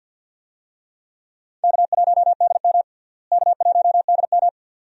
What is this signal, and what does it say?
Morse code, heard as a keyed tone, about 51 wpm, 700 Hz. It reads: K1DM K1DM
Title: USA Call Signs - 4 Characters - Repeated 2x - 50wpm